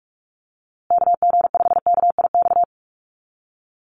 Morse code, heard as a keyed tone, about 44 wpm, 700 Hz. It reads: KG5KIX
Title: USA Call Signs Repeated 2x 45wpm